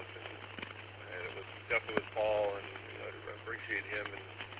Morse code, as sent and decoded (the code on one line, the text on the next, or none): none